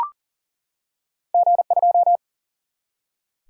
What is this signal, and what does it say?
Morse code, heard as a keyed tone, about 40 wpm, 700 Hz. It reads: G2
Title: Single-Letter Single-Number 40wpm